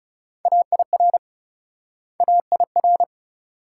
Morse code, heard as a keyed tone, about 39 wpm, 700 Hz. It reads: AIR AIR